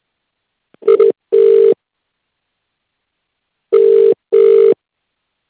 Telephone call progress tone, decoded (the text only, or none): ringback tone